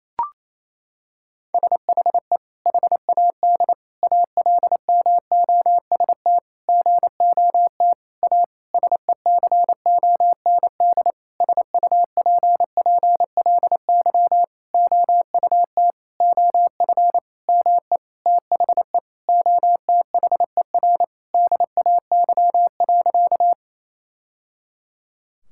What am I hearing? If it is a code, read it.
Morse code, heard as a keyed tone, about 28 wpm, 700 Hz. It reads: SHE HAD ALMOST GOT A SECOND SUPPLY OUT OF ME THE OTHER DAY.
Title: Sentences from Top 1000 Words - Review Part 5 - 28wpm